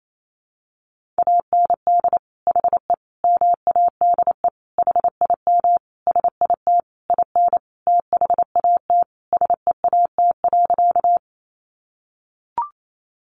Morse code, heard as a keyed tone, about 28 wpm, 700 Hz. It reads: AND HE MADE HIM SIT IN THAT SEAT.
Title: Sentences from Top 1500 Words 28wpm